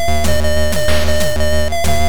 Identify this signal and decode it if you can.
There is no code.